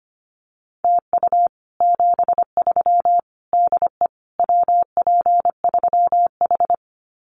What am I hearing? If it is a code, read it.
Morse code, heard as a keyed tone, about 25 wpm, 700 Hz. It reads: TU 73 DE WP3H
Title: Common QSO Phrases Encore 25wpm